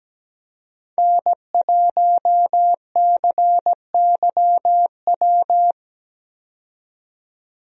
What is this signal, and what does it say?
Morse code, heard as a keyed tone, about 17 wpm, 700 Hz. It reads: N1CYW
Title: USA Call Signs - 5 Characters - 17wpm